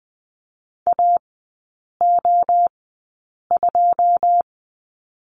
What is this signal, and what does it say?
Morse code, heard as a keyed tone, about 20 wpm, spaced as for 8 wpm, 700 Hz. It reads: AO2
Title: 114 Lesson 19e - Letter w - Three Characters 20wpm